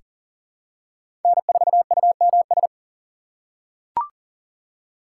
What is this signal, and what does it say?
Morse code, heard as a keyed tone, about 40 wpm, 700 Hz. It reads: N4UMS
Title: USA Call Signs - 5 Characters - 6x Character Spacing - 40wpm